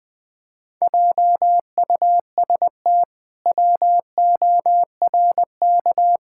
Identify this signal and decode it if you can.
Morse code, heard as a keyed tone, about 20 wpm, 700 Hz. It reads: JUST WORK
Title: Pangram Challenge v2 - 20wpm